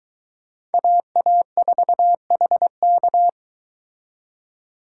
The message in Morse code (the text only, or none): AA4HK